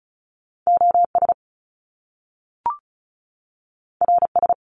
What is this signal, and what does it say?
Morse code, heard as a keyed tone, about 36 wpm, 700 Hz. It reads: OS RS
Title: Top Two Letter Combinations in Words 35wpm